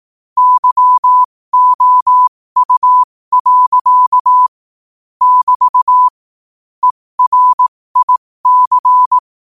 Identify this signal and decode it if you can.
Morse code, heard as a keyed tone, about 18 wpm, spaced as for 15 wpm, 1000 Hz. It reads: YOU. <BT> ERIC
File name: qotdfiles / 15wpm